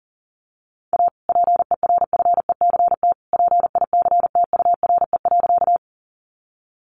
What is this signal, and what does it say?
Morse code, heard as a keyed tone, about 40 wpm, 700 Hz. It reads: A PERFECT PICTURE.
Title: Sentences from Top 1000 Words 40wpm